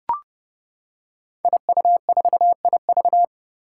Morse code, heard as a keyed tone, about 30 wpm, 700 Hz. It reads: IU4IV
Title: Five Challenging Letter-Number Combinations 30wpm